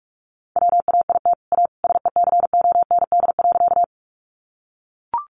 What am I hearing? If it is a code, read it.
Morse code, heard as a keyed tone, about 44 wpm, 700 Hz. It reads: WAIT A SECOND.
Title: Sentences from Top 500 Words 45wpm